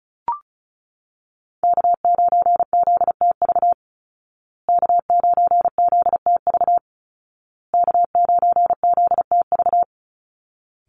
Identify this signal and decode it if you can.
Morse code, heard as a keyed tone, about 34 wpm, 700 Hz. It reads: K9ZTV K9ZTV K9ZTV
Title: Call Signs Repeated 2x - 35wpm